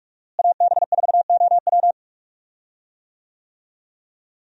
Morse code, heard as a keyed tone, about 42 wpm, 700 Hz. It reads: AB4OW
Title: USA Call Signs - 5 Characters - Repeated 3x - 45wpm